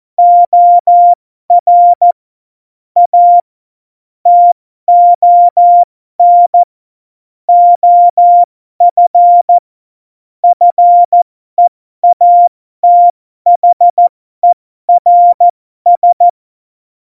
Morse code, heard as a keyed tone, about 14 wpm, spaced as for 12 wpm, 700 Hz. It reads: OR A TON OF FEATHERS
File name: Morsekurs / Witze